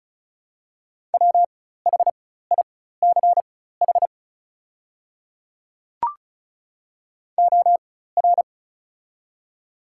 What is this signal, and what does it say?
Morse code, heard as a keyed tone, about 36 wpm, spaced as for 16 wpm, 700 Hz. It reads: WHICH OR
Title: 4x Character Spacing - Top 100 Words - 35wpm